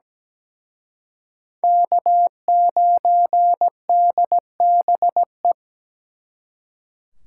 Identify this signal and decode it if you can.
Morse code, heard as a keyed tone, about 20 wpm, 700 Hz. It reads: K9DBE
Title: Speed Racing - USA 1x3 Call Signs 17wpm